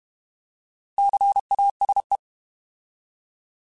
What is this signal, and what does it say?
Morse code, heard as a keyed tone, about 32 wpm, 800 Hz. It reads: CASE
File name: Alphabetical / Alphabetical mp3s / Cs @30